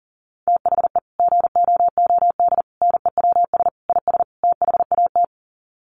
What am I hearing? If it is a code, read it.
Morse code, heard as a keyed tone, about 40 wpm, 700 Hz. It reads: THE GOOD NEWS IS THAT
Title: Sets of 5 Words 40wpm